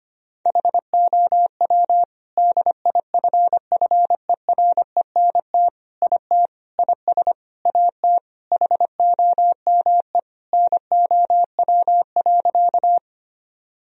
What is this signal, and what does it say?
Morse code, heard as a keyed tone, about 25 wpm, 700 Hz. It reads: HOW DIFFERENT IT IS AT HOME NOW.
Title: Sentences from Top 1000 Words - Review Part 3 - 25wpm